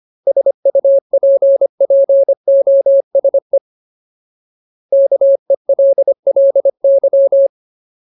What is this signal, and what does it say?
Morse code, heard as a keyed tone, about 25 wpm, 550 Hz. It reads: SUPPOSE KELLY
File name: Ultimate Word List / 25WPM Ultimate